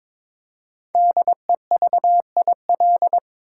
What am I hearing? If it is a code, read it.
Morse code, heard as a keyed tone, about 22 wpm, 700 Hz. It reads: DEVIL
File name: 5 Letter Words - First 3 Letters Given - 22wpm